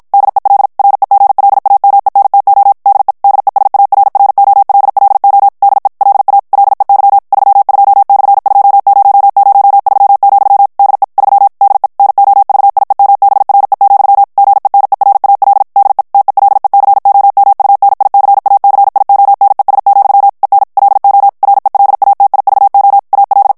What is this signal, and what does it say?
Morse code, heard as a keyed tone, about 54 wpm, 800 Hz. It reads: DEL REGLAMENTO DE DESARROLLO DE LA LEY 32/2003, DE 3 DE NOVIEMBRE, GENERAL DE TELECOMUNICACIONES, EN LO RELATIVO AL
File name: ESPAGNOL TEXTE EN CLAIR 2 VITESSE 50 MOTS MINUTES